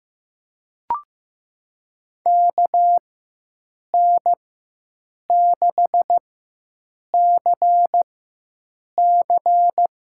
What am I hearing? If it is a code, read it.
Morse code, heard as a keyed tone, about 15 wpm, spaced as for 7 wpm, 700 Hz. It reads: KN6CC
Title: USA Call Signs - 5 Characters - 4x Character Spacing - 15wpm